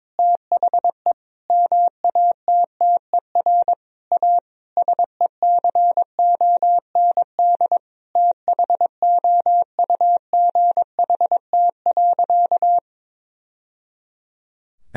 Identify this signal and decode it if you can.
Morse code, heard as a keyed tone, about 22 wpm, 700 Hz. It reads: THE MATTER A SECOND THOUGHT.
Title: Sentences from Top 1000 Words - Review Part 2 - 22wpm